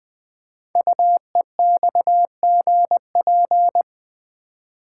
Morse code, heard as a keyed tone, about 20 wpm, 700 Hz. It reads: UEXGP
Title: Code Groups - 5 Letters - 20wpm